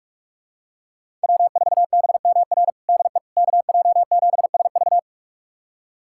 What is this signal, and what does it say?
Morse code, heard as a keyed tone, about 42 wpm, 700 Hz. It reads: W4BMR DE KJ7SV
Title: USA Callsign DE Callsign - 5 Characters - 45wpm